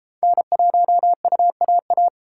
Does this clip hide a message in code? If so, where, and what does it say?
Yes: Morse code from 0.2 to 2.1 s, heard as a keyed tone, about 32 wpm, 700 Hz. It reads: N1UAA